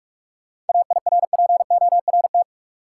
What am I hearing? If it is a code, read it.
Morse code, heard as a keyed tone, about 46 wpm, 700 Hz. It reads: AIRPORT
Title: Seven Letter Words - 45wpm